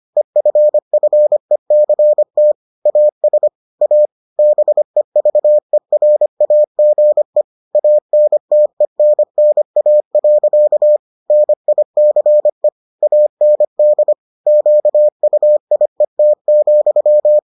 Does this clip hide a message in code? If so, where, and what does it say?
Yes: Morse code from 0.2 to 17.4 s, heard as a keyed tone, about 25 wpm, 600 Hz. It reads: EFFECT AS A BEVERAGE ANTENNA. NICE AND QUIET,